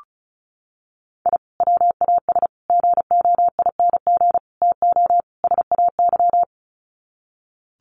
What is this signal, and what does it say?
Morse code, heard as a keyed tone, about 34 wpm, 700 Hz. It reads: I WAS GOING TO SAY
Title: Sets of 5 Words 35wpm